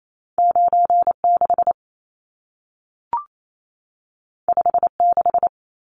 Morse code, heard as a keyed tone, about 28 wpm, 700 Hz. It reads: 96 56
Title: Numbers - 2-Digit 28wpm